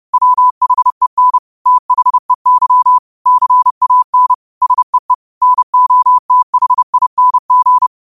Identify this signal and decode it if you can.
Morse code, heard as a keyed tone, about 30 wpm, 1000 Hz. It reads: WHEN THEY CAN SEE NOTHING
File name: qotdfiles / 30wpm